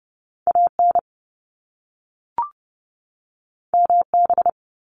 Morse code, heard as a keyed tone, about 30 wpm, 700 Hz. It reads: AN MB
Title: Top Two Letter Combinations in Words 30wpm